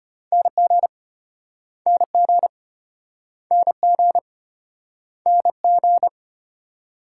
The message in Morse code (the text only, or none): NG NG NG NG